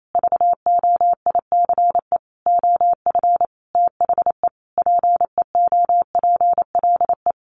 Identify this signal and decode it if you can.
Morse code, heard as a keyed tone, about 28 wpm, 700 Hz. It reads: VOICE OF THE PEOPLE
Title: Sentences from Top 1500 Words 28wpm